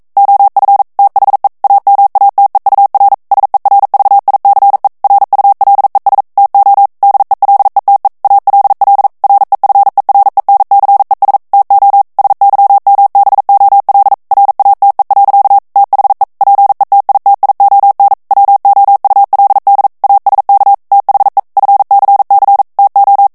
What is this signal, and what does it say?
Morse code, heard as a keyed tone, about 42 wpm, 800 Hz. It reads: OF THE AMATEUR SERVICE RULES TO DELETE ALL REFERENCES TO SYMBOL RATE. THE PETITION WOULD ASK THE FCC TO